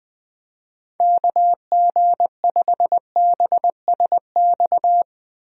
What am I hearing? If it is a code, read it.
Morse code, heard as a keyed tone, about 21 wpm, 700 Hz. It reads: KG5BSX
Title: USA Call Signs - 6 Characters - Repeated 3x - 20wpm